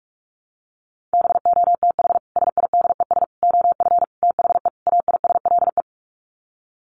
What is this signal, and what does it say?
Morse code, heard as a keyed tone, about 44 wpm, 700 Hz. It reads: BOTH SIDES OF THE AISLE
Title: Sets of 5 Words Encore2 45wpm